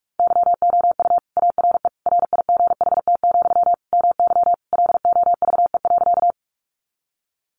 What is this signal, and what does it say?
Morse code, heard as a keyed tone, about 46 wpm, 700 Hz. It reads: YOU ARE RIGHT, MY LOVE.
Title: Sentences from Top 400 Words 45wpm